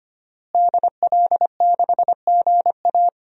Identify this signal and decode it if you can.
Morse code, heard as a keyed tone, about 25 wpm, 700 Hz. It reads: DL6GA
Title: DXCC-100 Call Signs - 5 Characters - Repeated 2x - 25wpm